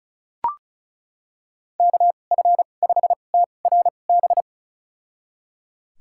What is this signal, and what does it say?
Morse code, heard as a keyed tone, about 36 wpm, spaced as for 25 wpm, 700 Hz. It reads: KF5TRB
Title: USA Call Signs - 6 Characters - 2x Character Spacing - 35wpm